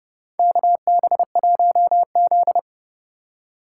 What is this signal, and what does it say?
Morse code, heard as a keyed tone, about 31 wpm, 700 Hz. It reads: KB1Z
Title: USA Call Signs - 4 Characters - 2x Character Spacing - 30wpm